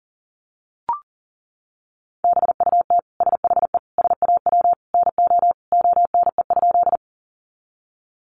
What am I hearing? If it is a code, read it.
Morse code, heard as a keyed tone, about 40 wpm, 700 Hz. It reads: BUT SHE SAW NO ONE?
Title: Sentences from Top 300 Words 40wpm